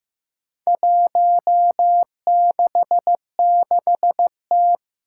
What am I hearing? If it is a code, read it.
Morse code, heard as a keyed tone, about 15 wpm, 700 Hz. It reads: 166T
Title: Four Challenging Letter-Number Combinations 15wpm